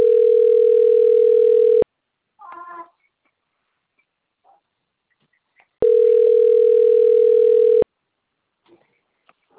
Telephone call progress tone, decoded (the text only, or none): ringback tone